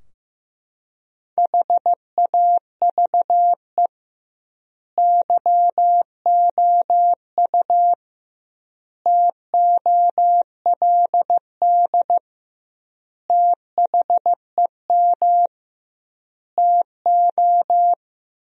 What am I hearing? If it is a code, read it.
Morse code, heard as a keyed tone, about 15 wpm, 700 Hz. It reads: HAVE YOU TOLD THEM TO
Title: 2x Word-Spacing - Sentences from Top 500 Words - 15wpm